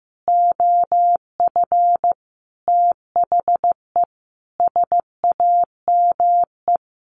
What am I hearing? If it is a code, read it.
Morse code, heard as a keyed tone, about 15 wpm, 700 Hz. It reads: OF THE SAME
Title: Sentences from Top 1400 Words 15wpm